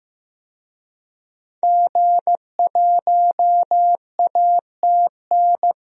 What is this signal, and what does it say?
Morse code, heard as a keyed tone, about 16 wpm, 700 Hz. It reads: G1ATN
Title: DXCC-100 Call Signs - 5 Characters - Repeated 3x - 15wpm